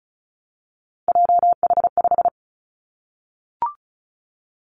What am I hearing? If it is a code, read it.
Morse code, heard as a keyed tone, about 36 wpm, 700 Hz. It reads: JH5
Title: Three Challenging Letter-Number Combinations 35wpm